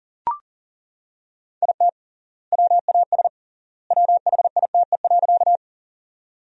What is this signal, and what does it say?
Morse code, heard as a keyed tone, about 40 wpm, 700 Hz. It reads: IT WAS WHITE.
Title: 3x Word-Spacing - Sentences from Top 300 Words - 40wpm